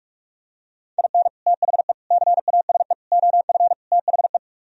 Morse code, heard as a keyed tone, about 48 wpm, 700 Hz. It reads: IN THE CASE OF THE